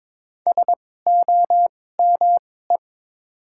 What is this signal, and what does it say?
Morse code, heard as a keyed tone, about 22 wpm, spaced as for 16 wpm, 700 Hz. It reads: SOME
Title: 2x Character Spacing - Top 200 Words - 22wpm